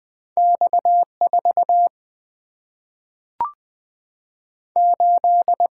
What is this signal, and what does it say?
Morse code, heard as a keyed tone, about 20 wpm, 700 Hz. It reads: X 4 8
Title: Single-Letter Single-Number 20wpm